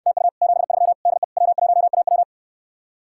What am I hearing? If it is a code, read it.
Morse code, heard as a keyed tone, about 69 wpm, 700 Hz. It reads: TU 73 DE W1AW